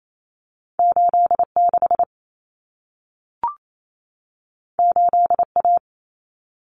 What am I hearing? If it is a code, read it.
Morse code, heard as a keyed tone, about 28 wpm, 700 Hz. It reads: 86 8A
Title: Common Cut Numbers - 2-Digit 28wpm